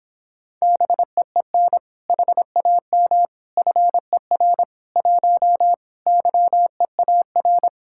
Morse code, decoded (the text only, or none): BEEN HAM FER 1 YEAR